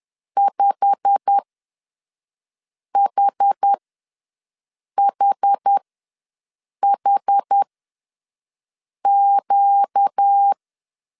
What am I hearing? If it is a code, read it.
Morse code, heard as a keyed tone, about 10 wpm, spaced as for 5 wpm, 800 Hz. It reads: 5HHHQ
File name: LEZIONI / 9 2867 ws 10wpm 4215